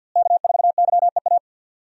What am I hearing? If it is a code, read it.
Morse code, heard as a keyed tone, about 50 wpm, 700 Hz. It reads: K4YEA
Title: USA Call Signs - 5 Characters - Repeated 3x - 50wpm